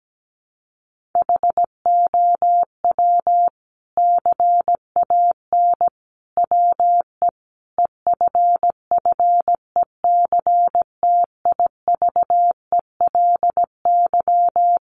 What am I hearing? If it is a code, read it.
Morse code, heard as a keyed tone, about 17 wpm, 700 Hz. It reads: HOW CAN WE EFFECTIVELY
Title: Sentences from Top 1400 Words 17wpm